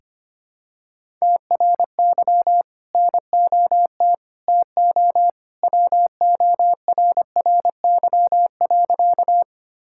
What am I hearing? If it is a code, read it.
Morse code, heard as a keyed tone, about 25 wpm, 700 Hz. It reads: TRY NOT TO WORRY.
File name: practice / 25wpm / Sentences with 2-Word Phrases - 25wpm